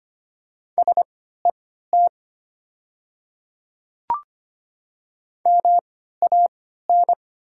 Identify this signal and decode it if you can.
Morse code, heard as a keyed tone, about 25 wpm, spaced as for 14 wpm, 700 Hz. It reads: SET MAN
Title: 3x Character Spacing - Top 200 Words - 25wpm